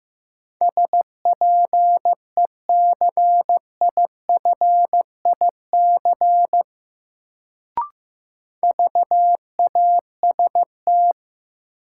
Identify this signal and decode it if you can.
Morse code, heard as a keyed tone, about 16 wpm, 700 Hz. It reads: SPECIFIC VAST
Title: Sentences from Top 1300 Words - New Word Drill Alt - 15wpm